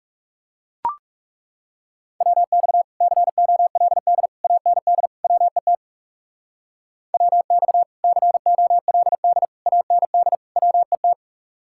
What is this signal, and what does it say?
Morse code, heard as a keyed tone, about 40 wpm, 700 Hz. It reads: WX COLD AND WET WX COLD AND WET